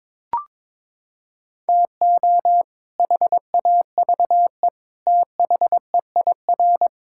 Morse code, heard as a keyed tone, about 21 wpm, 700 Hz. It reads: TO HAVE THEIR
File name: practice / 22wpm / Sets of 8 Words - 22wpm